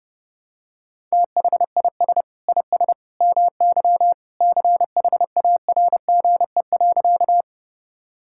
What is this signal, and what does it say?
Morse code, heard as a keyed tone, about 32 wpm, 700 Hz. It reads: THIS IS MY CHARGE.
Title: Sentences from Top 1000 Words - Review Part 6 - 30wpm